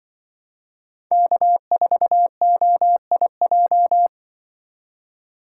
Morse code, heard as a keyed tone, about 24 wpm, 700 Hz. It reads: K4OIJ
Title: Speed Racing - USA Call Signs 20wpm